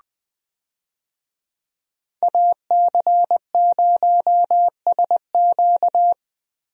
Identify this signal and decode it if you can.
Morse code, heard as a keyed tone, about 20 wpm, 700 Hz. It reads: AC0SQ
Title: USA Call Signs - 5 Characters - Repeated 3x - 20wpm